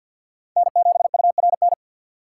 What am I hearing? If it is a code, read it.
Morse code, heard as a keyed tone, about 48 wpm, 700 Hz. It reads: N7URN
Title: USA Call Signs - 5 Characters - 6x Character Spacing - 50wpm